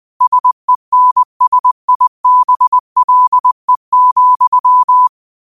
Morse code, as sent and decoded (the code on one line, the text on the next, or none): ... -. ... .. -... .-.. --..--
SENSIBLE,